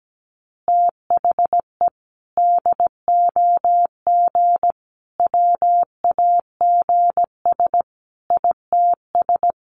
Morse code, heard as a keyed tone, about 17 wpm, 700 Hz. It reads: THE DOG WAGS ITS
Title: Easy Phrases 17wpm